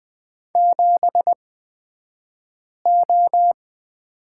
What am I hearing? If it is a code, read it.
Morse code, heard as a keyed tone, about 20 wpm, spaced as for 5 wpm, 700 Hz. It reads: 7O
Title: 235 Lesson 38d - Letter q - Two Characters 20wpm